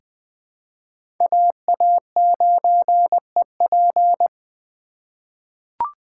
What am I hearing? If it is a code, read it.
Morse code, heard as a keyed tone, about 20 wpm, 700 Hz. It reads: AA9EP
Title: USA Call Signs - 5 Characters - 3x Character Spacing - 20wpm